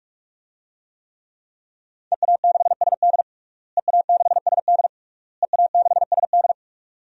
Morse code, heard as a keyed tone, about 46 wpm, 700 Hz. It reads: EA6SD EA6SD EA6SD
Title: DXCC-100 Call Signs - 5 Characters - Repeated 3x - 45wpm